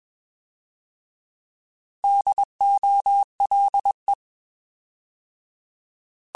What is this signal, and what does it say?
Morse code, heard as a keyed tone, about 21 wpm, 800 Hz. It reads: DOLE
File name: Alphabetical / Alphabetical mp3s / Ds @20